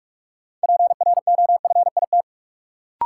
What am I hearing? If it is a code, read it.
Morse code, heard as a keyed tone, about 46 wpm, 700 Hz. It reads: PROFIT
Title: Sentences from Top 800 Words - New Word Drill - 45wpm